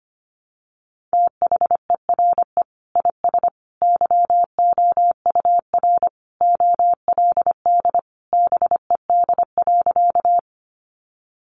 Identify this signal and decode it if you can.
Morse code, heard as a keyed tone, about 25 wpm, 700 Hz. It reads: THERE IS YOUR OLD BED.
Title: Sentences from Top 500 Words 25wpm